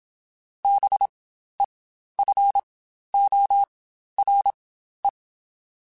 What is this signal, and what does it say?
Morse code, heard as a keyed tone, about 27 wpm, spaced as for 12 wpm, 800 Hz. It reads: BEFORE
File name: Spaced / Chuckles Spaced-mp3s / BrainMouth sp @25